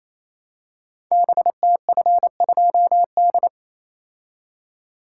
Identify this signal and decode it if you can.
Morse code, heard as a keyed tone, about 28 wpm, 700 Hz. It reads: BTF2D
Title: Code Groups - 5 Letter:Numbers - 28wpm